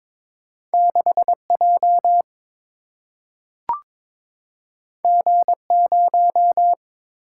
Morse code, heard as a keyed tone, about 22 wpm, 700 Hz. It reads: 6J G0
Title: Single-Letter Single-Number 22wpm